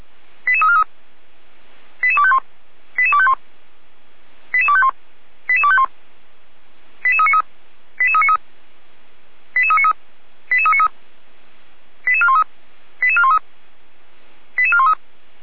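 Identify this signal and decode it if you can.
Five-tone selective call (ZVEI), heard as quick runs of five tones, tones of about 70 ms. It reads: 88343 88251 88251 88251 88251 88393 88393 88393 88393 88413 88413 88413
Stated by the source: ZVEI five-tone call 88343 received on an SDR.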